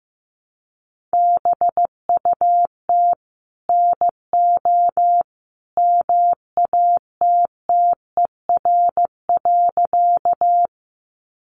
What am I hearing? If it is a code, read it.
Morse code, heard as a keyed tone, about 15 wpm, 700 Hz. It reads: BUT NO MATTER.
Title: Sentences from Top 700 Words 15wpm